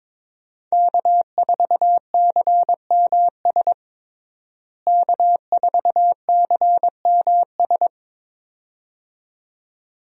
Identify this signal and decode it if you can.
Morse code, heard as a keyed tone, about 22 wpm, 700 Hz. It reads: K4CMS K4CMS
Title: USA Call Signs - 5 Characters - Repeated 2x - 22wpm